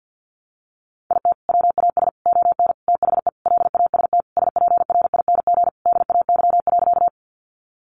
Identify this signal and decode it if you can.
Morse code, heard as a keyed tone, about 50 wpm, 700 Hz. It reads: IT WAS ON THE LAST SPRING DAY.
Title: Sentences from Top 1200 Words 50wpm